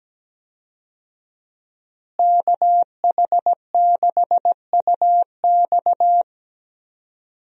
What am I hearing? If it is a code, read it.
Morse code, heard as a keyed tone, about 17 wpm, 700 Hz. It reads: KH6UX
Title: USA Call Signs - 5 Characters - Repeated 3x - 17wpm